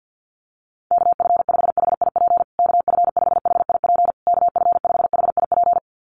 Morse code, heard as a keyed tone, about 50 wpm, 700 Hz. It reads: KF5HIP KF5HIP KF5HIP
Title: USA Call Signs Repeated 3x 50wpm